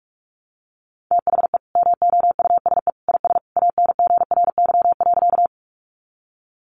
Morse code, heard as a keyed tone, about 47 wpm, 700 Hz. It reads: THE MOUSE IS ANGRY.
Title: Easy Phrases 45wpm